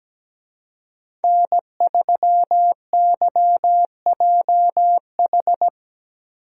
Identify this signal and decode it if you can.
Morse code, heard as a keyed tone, about 17 wpm, 700 Hz. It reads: N3YJH
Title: USA Call Signs - 5 Characters - Repeated 3x - 17wpm